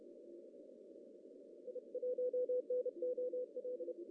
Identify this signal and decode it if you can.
Morse code, heard as a keyed tone, about 30 wpm, 500 Hz. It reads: I1NOL
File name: CW / cqww-cw-k5tr